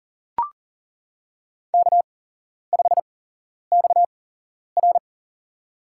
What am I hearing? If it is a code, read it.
Morse code, heard as a keyed tone, about 40 wpm, spaced as for 11 wpm, 700 Hz. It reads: K5XR
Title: USA Call Signs - 4 Characters - 8x Character Spacing - 40wpm